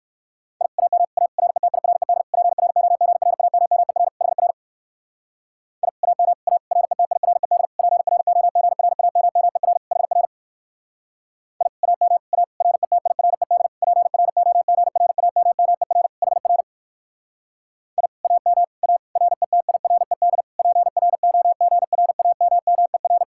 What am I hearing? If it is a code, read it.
Morse code, heard as a keyed tone, about 61 wpm, 700 Hz. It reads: I AM A RETIRED PROGRAMMER HR I AM A RETIRED PROGRAMMER HR I AM A RETIRED PROGRAMMER HR I AM A RETIRED PROGRAMMER